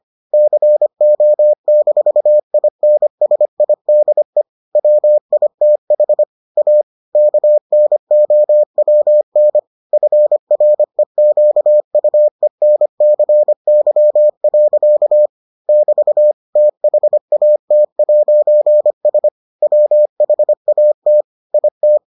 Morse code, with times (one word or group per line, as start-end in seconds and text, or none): 0.3-4.4 CO-INSIDE
4.8-6.2 WITH
6.6-6.8 A
7.2-9.6 KNOWN
9.9-15.3 FREQUENCY.
15.7-16.3 <BT>
16.6-19.3 THAT'S
19.6-21.2 WHAT
21.5-22.0 IT